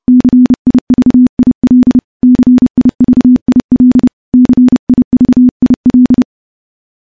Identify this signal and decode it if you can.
Morse code, heard as a keyed tone, about 30 wpm, 250 Hz. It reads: CIVIL CIVIL CIVIL
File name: morse / morse-training-30WPM